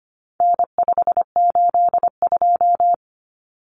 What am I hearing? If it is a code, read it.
Morse code, heard as a keyed tone, about 25 wpm, 700 Hz. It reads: N582
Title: Common Cut Numbers - 4-Digit 25wpm